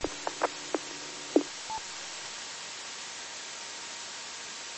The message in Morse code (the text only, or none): none